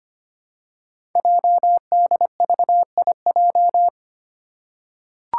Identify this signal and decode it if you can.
Morse code, heard as a keyed tone, about 25 wpm, 700 Hz. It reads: JDVIJ
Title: Code Groups - 5 Letter:Numbers - 25wpm